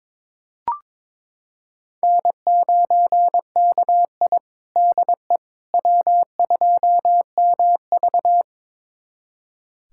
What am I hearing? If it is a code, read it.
Morse code, heard as a keyed tone, about 22 wpm, 700 Hz. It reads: N9KI DE W2MV
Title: USA Callsign DE Callsign - 4 Characters - 22wpm